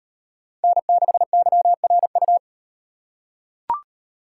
Morse code, heard as a keyed tone, about 38 wpm, 700 Hz. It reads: N6YRU